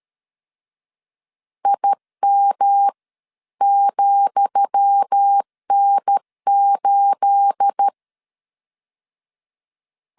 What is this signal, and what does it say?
Morse code, heard as a keyed tone, about 13 wpm, spaced as for 11 wpm, 800 Hz. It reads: IM,N8
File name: LEZIONI / 10 SIGNS / 10D COMA 12wpm 2544